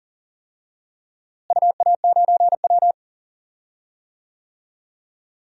Morse code, heard as a keyed tone, about 40 wpm, 700 Hz. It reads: UA9W
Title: DXCC-100 Call Signs - 4 Characters - Repeated 2x - 40wpm